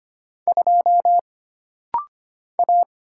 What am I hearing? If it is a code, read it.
Morse code, heard as a keyed tone, about 25 wpm, 700 Hz. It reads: 2 A
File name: Single Character (Letter:Number:Punctuation:Prosign) - Rapid-Fire - 25wpm